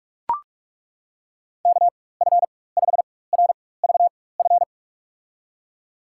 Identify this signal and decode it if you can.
Morse code, heard as a keyed tone, about 43 wpm, spaced as for 21 wpm, 700 Hz. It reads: KF5RVF
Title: USA Call Signs - 4x Character Spacing - 45wpm